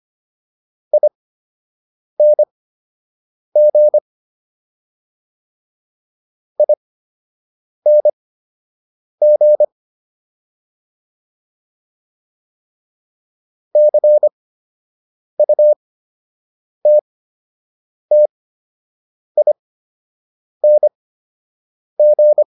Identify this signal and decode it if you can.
Morse code, heard as a keyed tone, about 25 wpm, spaced as for 7 wpm, 600 Hz. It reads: ING ING CUTTING